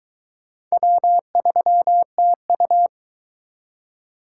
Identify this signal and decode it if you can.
Morse code, heard as a keyed tone, about 22 wpm, 700 Hz. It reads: W3TU